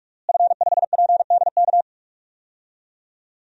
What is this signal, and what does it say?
Morse code, heard as a keyed tone, about 46 wpm, 700 Hz. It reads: F5PDK
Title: DXCC-100 Call Signs - 5 Characters - Repeated 3x - 45wpm